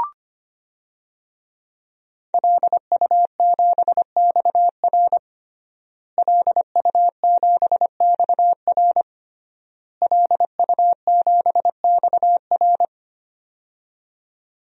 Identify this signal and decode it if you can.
Morse code, heard as a keyed tone, about 25 wpm, 700 Hz. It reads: LU7XR LU7XR LU7XR
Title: DXCC-100 Call Signs - 5 Characters - Repeated 3x - 25wpm